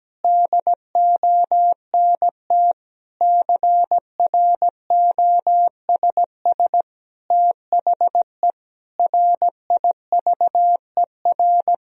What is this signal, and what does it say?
Morse code, heard as a keyed tone, about 17 wpm, 700 Hz. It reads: DONT CROSS THE RIVER
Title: Sentences from Top 1000 Words - Review Part 3 - 17wpm